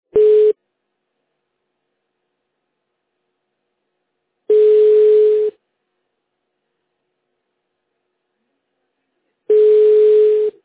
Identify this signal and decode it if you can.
Ringback tone (Europe, 425 Hz).